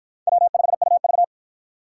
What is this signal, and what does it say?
Morse code, heard as a keyed tone, about 49 wpm, 700 Hz. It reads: W5UV